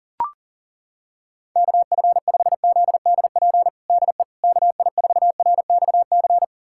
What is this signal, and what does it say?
Morse code, heard as a keyed tone, about 40 wpm, 700 Hz. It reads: KF5ZDP DE KI4RXC
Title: USA Callsign DE Callsign 40wpm